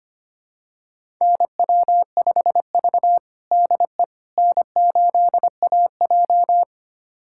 Morse code, heard as a keyed tone, about 25 wpm, 700 Hz. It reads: NW5V DE N8AJ
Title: USA Callsign DE Callsign - 4 Characters - 25wpm